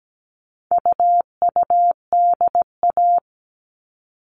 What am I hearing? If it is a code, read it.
Morse code, heard as a keyed tone, about 17 wpm, 700 Hz. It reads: UUDA
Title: All Cut Numbers - 4-Digit 17wpm